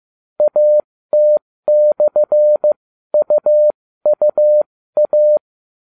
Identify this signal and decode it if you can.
Morse code, heard as a keyed tone, about 15 wpm, spaced as for 13 wpm, 600 Hz. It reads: AT/UUA